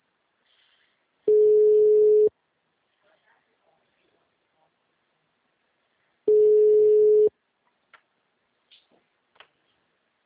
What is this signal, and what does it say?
Ringback tone (Europe, 425 Hz).